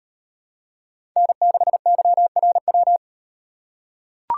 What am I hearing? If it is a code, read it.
Morse code, heard as a keyed tone, about 36 wpm, 700 Hz. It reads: N6YRW